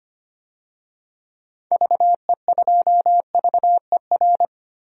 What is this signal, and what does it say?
Morse code, heard as a keyed tone, about 25 wpm, 700 Hz. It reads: VE2VER
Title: DXCC-100 Call Signs - 6 Characters - 25wpm